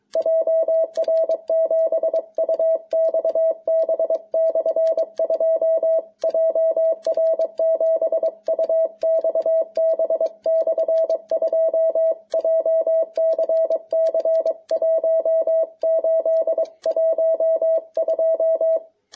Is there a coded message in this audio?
Yes: Morse code from 0.2 to 18.8 s, heard as a keyed tone, about 22 wpm, 660 Hz. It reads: JR7UXB/2 JR7UXB/2 JCC 1812